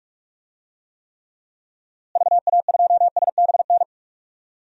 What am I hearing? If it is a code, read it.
Morse code, heard as a keyed tone, about 46 wpm, 700 Hz. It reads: VA2SBN
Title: DXCC-100 Call Signs - 6 Characters - Repeated 2x - 45wpm